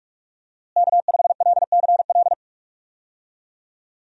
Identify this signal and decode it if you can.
Morse code, heard as a keyed tone, about 46 wpm, 700 Hz. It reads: K5LCL
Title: USA Call Signs - 5 Characters - Repeated 3x - 45wpm